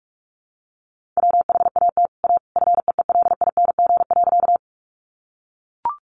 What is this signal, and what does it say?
Morse code, heard as a keyed tone, about 44 wpm, 700 Hz. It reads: WHAT A FEELING.
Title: Sentences from Top 1200 Words 45wpm